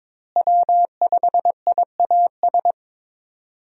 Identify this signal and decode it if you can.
Morse code, heard as a keyed tone, about 22 wpm, 700 Hz. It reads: W5IAS
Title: USA Call Signs - 5 Characters - Repeated 3x - 22wpm